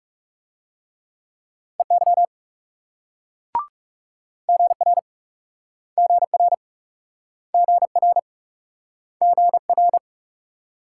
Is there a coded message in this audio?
Yes: Morse code from 1.8 to 10.0 s, heard as a keyed tone, about 41 wpm, 700 Hz. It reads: EY GR GR GR GR